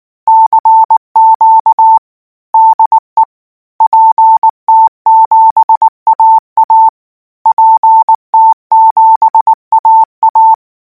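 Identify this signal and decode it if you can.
Morse code, heard as a keyed tone, about 19 wpm, 890 Hz. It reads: CQ DE PT7AA PT7AA
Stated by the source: Morse code practice text at 18 wpm, from Portal do CW.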